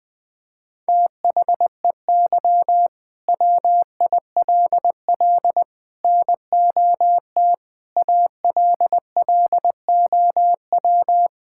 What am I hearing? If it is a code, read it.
Morse code, heard as a keyed tone, about 20 wpm, 700 Hz. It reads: THEY WILL NOT ALLOW
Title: Sentences from Top 1000 Words - Review Part 5 - 20wpm